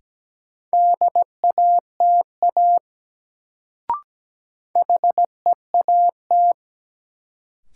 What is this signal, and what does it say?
Morse code, heard as a keyed tone, about 17 wpm, 700 Hz. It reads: DATA HEAT
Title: Sentences from Top 1000 Words - Review Part 3 - New Word Drill - 17wpm